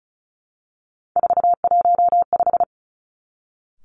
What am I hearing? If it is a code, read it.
Morse code, heard as a keyed tone, about 36 wpm, 700 Hz. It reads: 415
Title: Three Challenging Letter-Number Combinations 35wpm